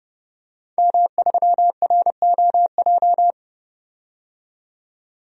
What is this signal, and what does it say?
Morse code, heard as a keyed tone, about 30 wpm, 700 Hz. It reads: M3ROJ
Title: DXCC-100 Call Signs - 5 Characters - 30wpm